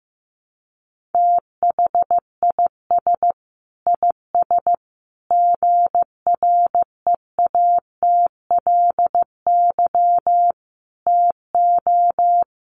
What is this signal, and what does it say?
Morse code, heard as a keyed tone, about 15 wpm, 700 Hz. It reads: THIS IS GREATLY TO
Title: Sentences from Top 1400 Words 15wpm